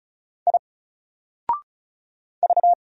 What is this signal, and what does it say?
Morse code, heard as a keyed tone, about 34 wpm, 700 Hz. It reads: I V
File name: Single-Letter-Number-35wpm